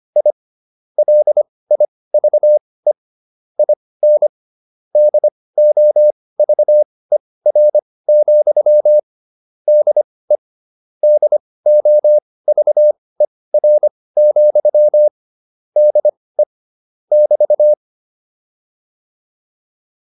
Morse code, heard as a keyed tone, about 25 wpm, spaced as for 18 wpm, 600 Hz. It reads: I LIVE IN DOVER, DE DOVER, DE <BT>
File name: QSO308 18